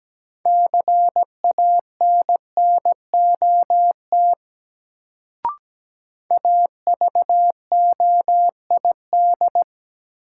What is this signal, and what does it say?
Morse code, heard as a keyed tone, about 17 wpm, 700 Hz. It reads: CANNOT AVOID
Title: Sentences from Top 1000 Words - Review Part 6 - New Word Drill - 17wpm